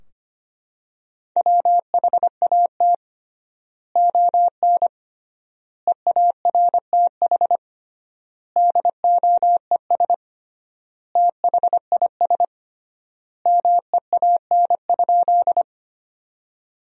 Morse code, heard as a keyed tone, about 25 wpm, 700 Hz. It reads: WHAT ON EARTH DOES THIS MEAN?
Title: 3x Word-Spacing - Sentences from Top 200 Words - 25wpm